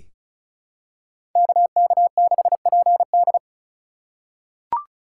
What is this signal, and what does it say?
Morse code, heard as a keyed tone, about 34 wpm, 700 Hz. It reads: KK6PD
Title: USA Call Signs - 4x Character Spacing - 35wpm